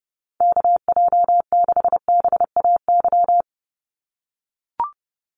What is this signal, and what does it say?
Morse code, heard as a keyed tone, about 30 wpm, 700 Hz. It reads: KJ6BAY
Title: USA Call Signs 30wpm